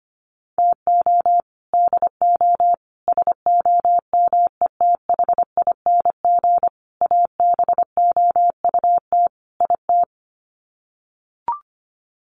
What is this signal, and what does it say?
Morse code, heard as a keyed tone, about 25 wpm, 700 Hz. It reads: TO DO SOMETHING ABOUT IT
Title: Sets of 5 Words 25wpm